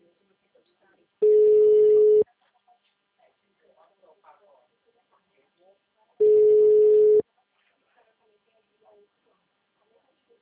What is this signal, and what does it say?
Ringback tone (Europe, 425 Hz), with speech.
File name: out- -2009- - - .54860